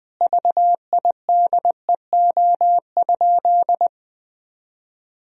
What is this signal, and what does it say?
Morse code, heard as a keyed tone, about 20 wpm, 700 Hz. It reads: VIDEO?
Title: Sentences from Top 1000 Words - Review Part 4 - 20wpm